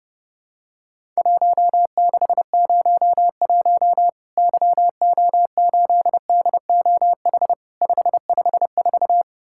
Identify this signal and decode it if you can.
Morse code, heard as a keyed tone, about 30 wpm, 700 Hz. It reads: 1601 YO8DOH 554
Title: Worked All Europe DX Contest - QTC Exchange - 30wpm